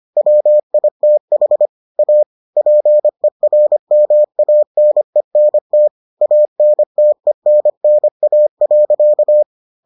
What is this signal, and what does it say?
Morse code, heard as a keyed tone, about 25 wpm, 600 Hz. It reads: WITH A PERMANENT ANTENNA.